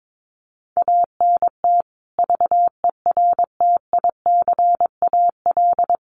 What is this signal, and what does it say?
Morse code, heard as a keyed tone, about 22 wpm, 700 Hz. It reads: ANT VERTICAL
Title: Common QSO Phrases 22wpm